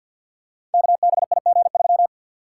Morse code, heard as a keyed tone, about 47 wpm, 700 Hz. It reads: KBIG3
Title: Code Groups - 5 Letter:Numbers - 50wpm